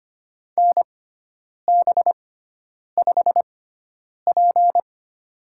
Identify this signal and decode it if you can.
Morse code, heard as a keyed tone, about 25 wpm, spaced as for 9 wpm, 700 Hz. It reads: NB5P
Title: USA Call Signs - 6x Character Spacing - 25wpm